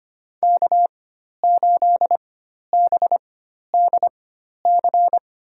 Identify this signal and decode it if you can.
Morse code, heard as a keyed tone, about 25 wpm, spaced as for 12 wpm, 700 Hz. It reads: K8BDC
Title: USA Call Signs - 5 Characters - 4x Character Spacing - 25wpm